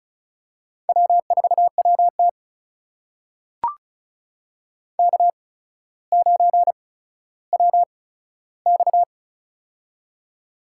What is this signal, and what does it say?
Morse code, heard as a keyed tone, about 36 wpm, spaced as for 15 wpm, 700 Hz. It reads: W4WT K9WX
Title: USA Call Signs - 4 Characters - 8x Character Spacing - 35wpm